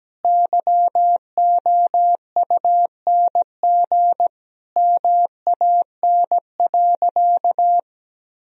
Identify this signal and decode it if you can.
Morse code, heard as a keyed tone, about 17 wpm, 700 Hz. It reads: YOUNG MAN.
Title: Sentences from Top 400 Words and Abbreviations 17wpm